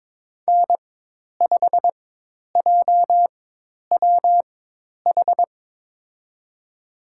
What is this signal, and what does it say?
Morse code, heard as a keyed tone, about 22 wpm, spaced as for 10 wpm, 700 Hz. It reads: N5JWH
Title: USA Call Signs - 5 Characters - 4x Character Spacing - 22wpm